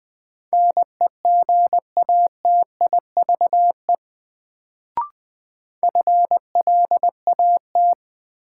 Morse code, heard as a keyed tone, about 21 wpm, 700 Hz. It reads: NEGATIVE FLAT
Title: Sentences from Top 1000 Words - New Word Drill - 20wpm